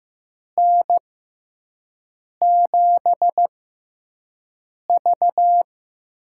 Morse code, heard as a keyed tone, about 15 wpm, spaced as for 5 wpm, 700 Hz. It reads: N7V
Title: USA Call Signs - 6x Character Spacing - 15wpm